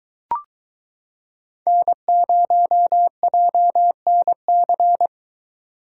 Morse code, heard as a keyed tone, about 22 wpm, 700 Hz. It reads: N0JNC